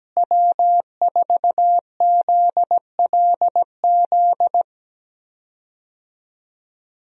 Morse code, heard as a keyed tone, about 17 wpm, 700 Hz. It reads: W4ZLZ
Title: USA Call Signs - 5 Characters - Repeated 3x - 17wpm